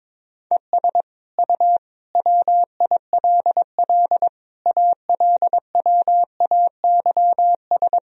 Morse code, heard as a keyed tone, about 21 wpm, 700 Hz. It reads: ES U WILL ALWAYS
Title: Sentences from Top 400 Words and Abbreviations 22wpm